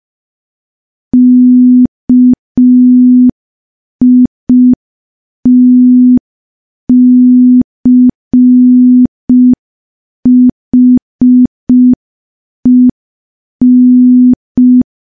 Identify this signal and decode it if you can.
Morse code, heard as a keyed tone, about 5 wpm, 250 Hz. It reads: KITCHEN
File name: morse / morse-training-05WPM